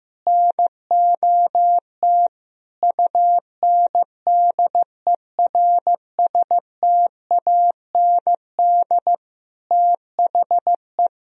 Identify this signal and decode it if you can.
Morse code, heard as a keyed tone, about 15 wpm, 700 Hz. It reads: NOT UNDERSTAND THE
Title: Sentences from Top 1000 Words - Review Part 3 - 15wpm